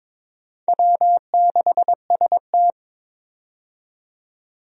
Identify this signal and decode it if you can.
Morse code, heard as a keyed tone, about 22 wpm, 700 Hz. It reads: W6ST
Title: USA Call Signs - 4 Characters - 22wpm